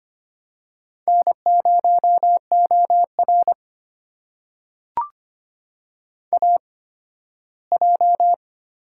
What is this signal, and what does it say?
Morse code, heard as a keyed tone, about 25 wpm, spaced as for 15 wpm, 700 Hz. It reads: N0OR AJ
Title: USA Call Signs - 4 Characters - 8x Character Spacing - 25wpm